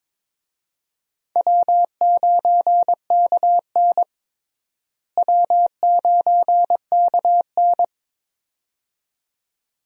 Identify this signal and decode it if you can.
Morse code, heard as a keyed tone, about 22 wpm, 700 Hz. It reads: W9KN W9KN
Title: USA Call Signs - 4 Characters - Repeated 2x - 22wpm